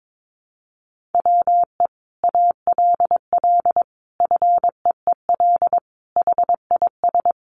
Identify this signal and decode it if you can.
Morse code, heard as a keyed tone, about 22 wpm, 700 Hz. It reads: WE ALL FEEL HIS
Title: Sentences from Top 1400 Words 22wpm